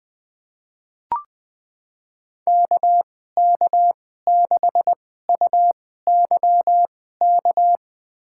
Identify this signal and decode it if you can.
Morse code, heard as a keyed tone, about 20 wpm, spaced as for 14 wpm, 700 Hz. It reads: KK6UYK
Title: USA Call Signs - 2x Character Spacing - 20wpm